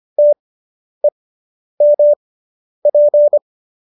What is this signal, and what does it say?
Morse code, heard as a keyed tone, about 25 wpm, spaced as for 10 wpm, 590 Hz. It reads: TEMP
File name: QSO220 10